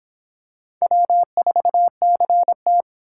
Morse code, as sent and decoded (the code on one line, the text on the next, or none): .-- ....- -.-. -
W4CT